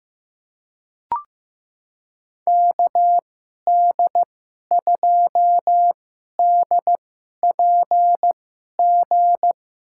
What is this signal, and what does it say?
Morse code, heard as a keyed tone, about 15 wpm, spaced as for 11 wpm, 700 Hz. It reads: KD2DPG
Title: USA Call Signs - 6 Characters - 2x Character Spacing - 15wpm